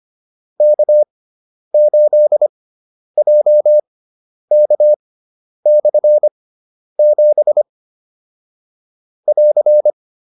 Morse code, heard as a keyed tone, about 25 wpm, spaced as for 10 wpm, 600 Hz. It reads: K8JK/7 <AR>